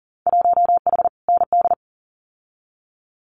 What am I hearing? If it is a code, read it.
Morse code, heard as a keyed tone, about 42 wpm, 700 Hz. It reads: 1H ND
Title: Winter Field Day Exchanges 40wpm